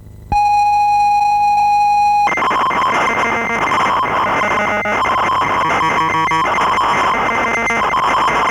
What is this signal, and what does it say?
ZX Spectrum tape loading.
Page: Как звучали игры ZX Spectrum